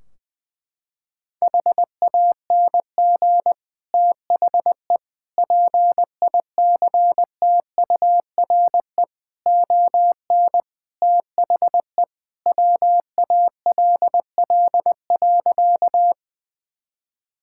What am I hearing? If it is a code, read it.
Morse code, heard as a keyed tone, about 21 wpm, 700 Hz. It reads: HANG THE PICTURE ON THE WALL.
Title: Sentences from Top 1600 Words 20wpm